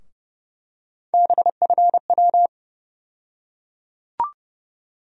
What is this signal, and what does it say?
Morse code, heard as a keyed tone, about 30 wpm, 700 Hz. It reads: BFW
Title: USA Call Sign Postfixes - 3 Letters 30wpm